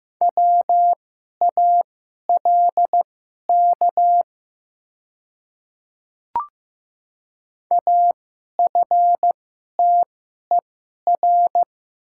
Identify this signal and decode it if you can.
Morse code, heard as a keyed tone, about 15 wpm, spaced as for 11 wpm, 700 Hz. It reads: WALK AFTER
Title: 2x Character Spacing - Top 500 Words - 15wpm